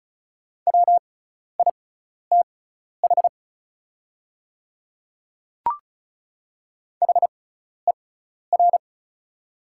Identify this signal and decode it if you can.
Morse code, heard as a keyed tone, about 36 wpm, spaced as for 12 wpm, 690 Hz. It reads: WITH HER
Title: 6x Character Spacing - Top 100 Words - 35wpm